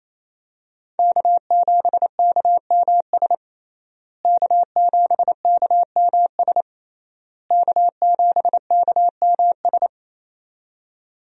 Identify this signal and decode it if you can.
Morse code, heard as a keyed tone, about 28 wpm, 700 Hz. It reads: K7KMS K7KMS K7KMS
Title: USA Call Signs - 5 Characters - Repeated 3x - 28wpm